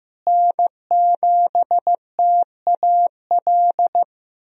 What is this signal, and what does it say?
Morse code, heard as a keyed tone, about 15 wpm, 700 Hz. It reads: N7TAL
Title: USA Call Signs - 5 Characters - Repeated 2x - 15wpm